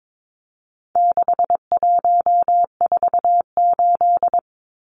Morse code, heard as a keyed tone, about 22 wpm, 700 Hz. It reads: 6148
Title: Numbers - 4-Digit 22wpm